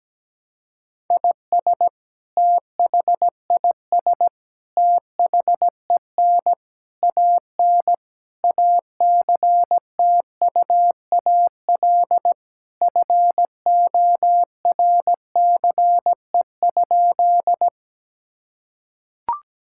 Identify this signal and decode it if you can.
Morse code, heard as a keyed tone, about 17 wpm, 700 Hz. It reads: IS THIS THEN AN ACTUAL FORCE?
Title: Sentences from Top 1000 Words - Review Part 5 - 17wpm